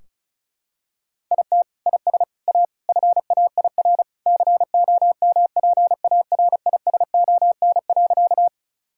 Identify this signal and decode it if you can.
Morse code, heard as a keyed tone, about 36 wpm, 700 Hz. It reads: IT IS A FAIR COMPARISON.
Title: Sentences from Top 1600 Words 35wpm